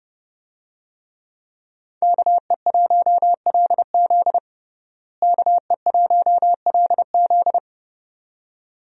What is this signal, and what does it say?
Morse code, heard as a keyed tone, about 30 wpm, 700 Hz. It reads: KE1LZ KE1LZ
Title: USA Call Signs - 5 Characters - Repeated 2x - 30wpm